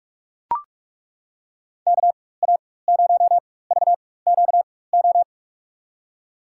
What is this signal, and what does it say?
Morse code, heard as a keyed tone, about 42 wpm, spaced as for 21 wpm, 700 Hz. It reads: KA0VQO
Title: USA Call Signs - 6 Characters - 4x Character Spacing - 45wpm